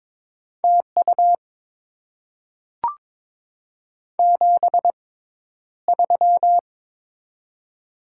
Morse code, heard as a keyed tone, about 23 wpm, spaced as for 11 wpm, 690 Hz. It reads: TU 73
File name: Top 20 QSO Elements - 6x Character Spacing - 22wpm